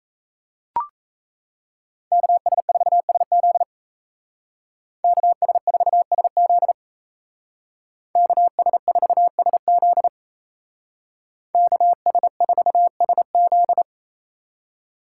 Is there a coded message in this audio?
Yes: Morse code from 2.1 to 13.8 s, heard as a keyed tone, about 36 wpm, 700 Hz. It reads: KS4SZ KS4SZ KS4SZ KS4SZ